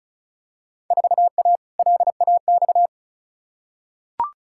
Morse code, heard as a keyed tone, about 36 wpm, 700 Hz. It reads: 4A LAX
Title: Field Day Exchanges - 35wpm